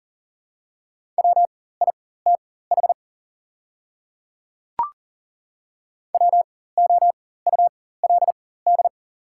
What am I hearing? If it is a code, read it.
Morse code, heard as a keyed tone, about 40 wpm, spaced as for 19 wpm, 690 Hz. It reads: WITH WOULD
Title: 4x Character Spacing - Top 500 Words - 40wpm